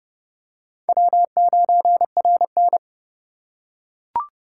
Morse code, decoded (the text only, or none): W9RN